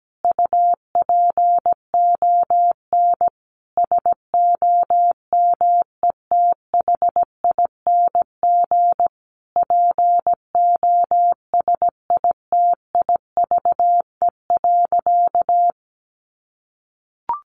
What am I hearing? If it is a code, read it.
Morse code, heard as a keyed tone, about 17 wpm, 700 Hz. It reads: UPON SOMETHING POSITIVE.
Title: Sentences from Top 1400 Words 17wpm